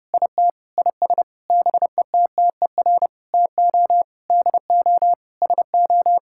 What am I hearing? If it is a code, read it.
Morse code, heard as a keyed tone, about 32 wpm, 700 Hz. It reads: IT IS BETTER TO DO SO
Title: Sentences from Top 1600 Words 30wpm